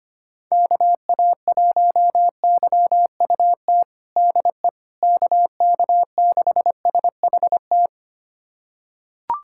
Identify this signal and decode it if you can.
Morse code, heard as a keyed tone, about 25 wpm, 700 Hz. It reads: KA1YUT DE KK6SHT
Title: USA Callsign DE Callsign 25wpm